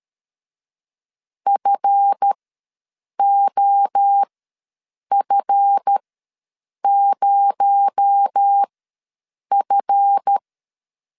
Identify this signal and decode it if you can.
Morse code, heard as a keyed tone, about 13 wpm, spaced as for 7 wpm, 800 Hz. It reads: FOF0F
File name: LEZIONI / 7 GF9Z / 7 GF ws 12wpm 3140